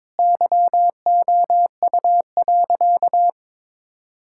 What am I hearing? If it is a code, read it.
Morse code, heard as a keyed tone, about 22 wpm, 700 Hz. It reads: YOU.
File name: practice / 22wpm / Easy Sentences with Hints - 22wpm